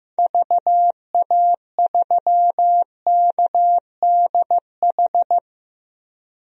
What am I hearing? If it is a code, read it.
Morse code, heard as a keyed tone, about 15 wpm, 700 Hz. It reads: VA3KDH
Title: DXCC-100 Call Signs - 6 Characters - Repeated 2x - 15wpm